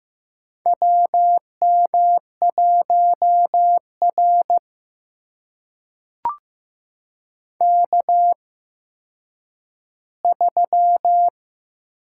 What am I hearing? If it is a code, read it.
Morse code, heard as a keyed tone, about 15 wpm, spaced as for 9 wpm, 700 Hz. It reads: WM1R K3
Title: USA Call Signs - 4 Characters - 8x Character Spacing - 15wpm